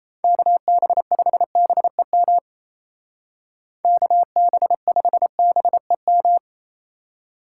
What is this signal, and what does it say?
Morse code, heard as a keyed tone, about 33 wpm, 700 Hz. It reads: KB5BEM KB5BEM
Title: Speed Racing - USA 2x3 Call Signs 28wpm